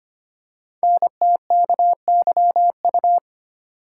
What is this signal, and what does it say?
Morse code, heard as a keyed tone, about 25 wpm, 700 Hz. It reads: NTKYU
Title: Code Groups - 5 Letters - 25wpm